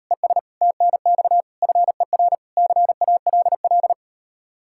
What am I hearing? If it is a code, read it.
Morse code, heard as a keyed tone, about 38 wpm, 700 Hz. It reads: ES TNX FER CALL